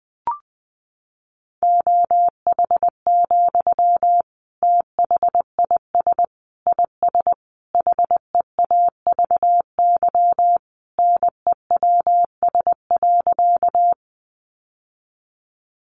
Morse code, heard as a keyed tone, about 20 wpm, 700 Hz. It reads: OH, THIS IS HEAVY NEWS.
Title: Sentences from Top 1100 Words 20wpm